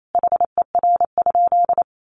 Morse code, heard as a keyed tone, about 28 wpm, 700 Hz. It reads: HER?
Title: Sentences from Top 900 Words 28wpm